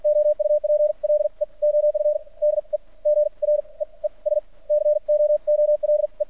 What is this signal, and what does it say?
Morse code, heard as a keyed tone, about 50 wpm, spaced as for 37 wpm, 600 Hz. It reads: OUPLE OF DEGREES COOLE
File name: archived / CRSnet